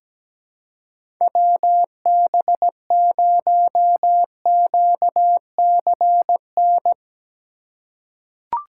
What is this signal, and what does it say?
Morse code, heard as a keyed tone, about 17 wpm, 700 Hz. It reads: WB0QCN
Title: USA Call Signs - 6 Characters - 2x Character Spacing - 17wpm